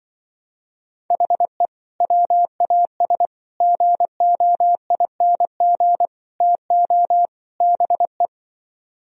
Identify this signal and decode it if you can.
Morse code, heard as a keyed tone, about 25 wpm, 700 Hz. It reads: HE WAS GOING TO BE